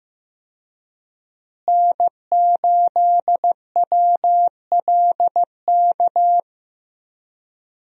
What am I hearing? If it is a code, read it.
Morse code, heard as a keyed tone, about 15 wpm, 700 Hz. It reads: N8WLK
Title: USA Call Signs - 5 Characters - Repeated 3x - 15wpm